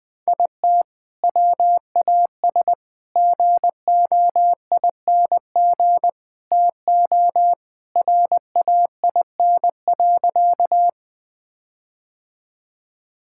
Morse code, heard as a keyed tone, about 21 wpm, 700 Hz. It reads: IT WAS GOING TO RAIN.
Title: Sentences with 4-Word Phrases - 20wpm